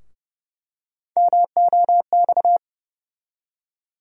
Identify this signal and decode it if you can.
Morse code, heard as a keyed tone, about 32 wpm, 700 Hz. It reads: MOX
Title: USA Call Sign Postfixes - 3 Letters 30wpm